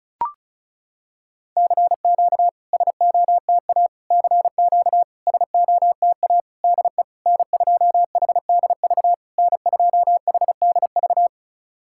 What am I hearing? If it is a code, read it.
Morse code, heard as a keyed tone, about 36 wpm, 690 Hz. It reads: CQ SOTA CQ SOTA DE N2HDV N2HDV
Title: SOTA QSO Practice 35wpm